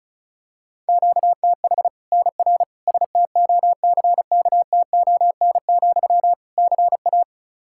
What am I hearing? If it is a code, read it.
Morse code, heard as a keyed tone, about 34 wpm, 700 Hz. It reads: QTH NR STOCKTON, CA
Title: Practice QSO - Exchange 1 - Beginning - 35wpm